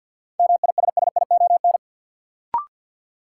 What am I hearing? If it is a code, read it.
Morse code, heard as a keyed tone, about 53 wpm, 700 Hz. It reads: MISSION
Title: Sentences from Top 1600 Words - New Word Drill - 50wpm